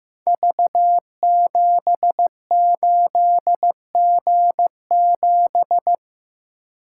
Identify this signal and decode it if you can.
Morse code, heard as a keyed tone, about 15 wpm, 700 Hz. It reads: V78G7
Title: Code Groups - 5 Letter:Numbers - 15wpm